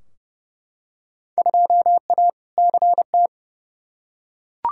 Morse code, heard as a keyed tone, about 30 wpm, 700 Hz. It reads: 2A CT
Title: Field Day Exchanges Encore - 30wpm